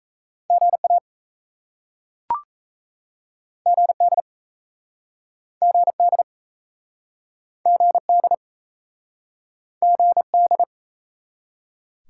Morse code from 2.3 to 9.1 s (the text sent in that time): GD GD GD